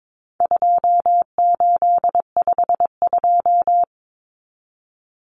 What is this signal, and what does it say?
Morse code, heard as a keyed tone, about 22 wpm, 700 Hz. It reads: 2852
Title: Common Cut Numbers - 4-Digit 22wpm